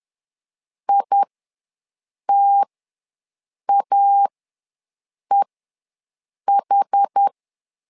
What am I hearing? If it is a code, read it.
Morse code, heard as a keyed tone, about 10 wpm, spaced as for 6 wpm, 800 Hz. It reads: ITAEH
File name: LEZIONI / 2 UHDN / 2 UHDN ws 10wpm 2044